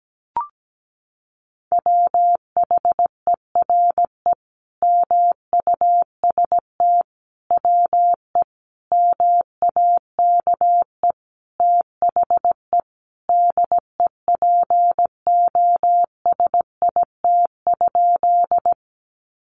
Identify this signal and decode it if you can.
Morse code, heard as a keyed tone, about 17 wpm, 700 Hz. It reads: WHERE MUST WE MAKE THE DEPOSIT?
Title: Sentences from Top 1300 Words 17wpm